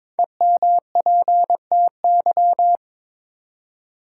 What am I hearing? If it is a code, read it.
Morse code, heard as a keyed tone, about 22 wpm, 700 Hz. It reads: EMPTY
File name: 5 Letter Words - First 3 Letters Given - 22wpm